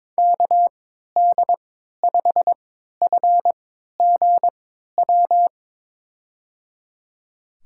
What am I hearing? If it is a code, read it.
Morse code, heard as a keyed tone, about 22 wpm, spaced as for 13 wpm, 700 Hz. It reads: KD5FGW
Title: USA Call Signs - 6 Characters - 3x Character Spacing - 22wpm